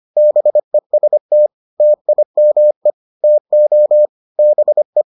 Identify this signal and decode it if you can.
Morse code, heard as a keyed tone, about 25 wpm, 600 Hz. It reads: BEST TIME TO BE